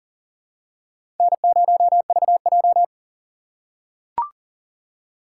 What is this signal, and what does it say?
Morse code, heard as a keyed tone, about 40 wpm, 700 Hz. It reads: N0VJ
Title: USA Call Signs - 4 Characters - 4x Character Spacing - 40wpm